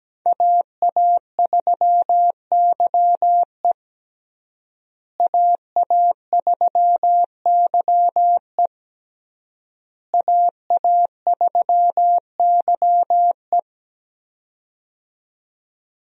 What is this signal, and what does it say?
Morse code, heard as a keyed tone, about 17 wpm, 700 Hz. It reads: AA3YE AA3YE AA3YE
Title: USA Call Signs - 5 Characters - Repeated 3x - 17wpm